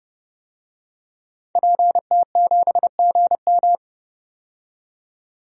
Morse code, heard as a keyed tone, about 30 wpm, 700 Hz. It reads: PT7GM
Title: DXCC-100 Call Signs - 5 Characters - 30wpm